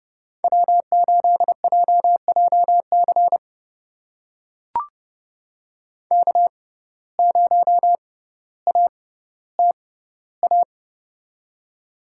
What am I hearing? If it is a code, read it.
Morse code, heard as a keyed tone, about 30 wpm, spaced as for 15 wpm, 700 Hz. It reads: W8JJC K0ATA
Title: USA Call Signs - 5 Characters - 6x Character Spacing - 30wpm